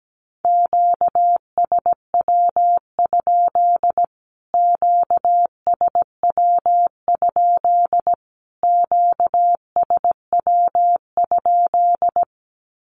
Morse code, heard as a keyed tone, about 17 wpm, 700 Hz. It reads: QSW? QSW? QSW?
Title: Q-Codes Repeated 3x 17wpm